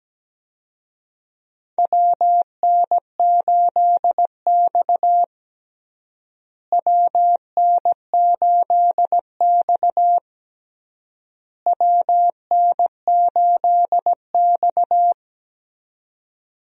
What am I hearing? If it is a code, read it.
Morse code, heard as a keyed tone, about 17 wpm, 700 Hz. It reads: WN8X WN8X WN8X
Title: USA Call Signs - 4 Characters - Repeated 3x - 17wpm